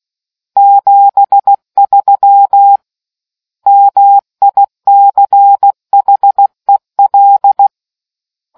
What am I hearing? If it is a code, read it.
Morse code, heard as a keyed tone, about 16 wpm, 800 Hz. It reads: 73 MICHEL